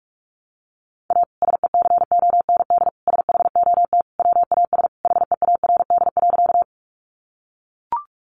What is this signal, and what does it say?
Morse code, heard as a keyed tone, about 44 wpm, 700 Hz. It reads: A SECOND SHOT WAS HEARD.
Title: Sentences from Top 1100 Words 45wpm